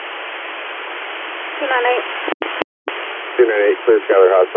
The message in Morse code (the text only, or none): none